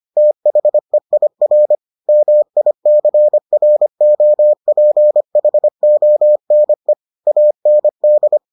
Morse code, heard as a keyed tone, about 25 wpm, 600 Hz. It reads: THEIR MICROPHONE AND